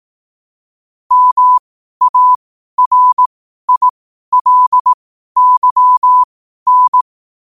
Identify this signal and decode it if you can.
Morse code, heard as a keyed tone, about 18 wpm, spaced as for 12 wpm, 1000 Hz. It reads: MARILYN
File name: qotdfiles / 12wpm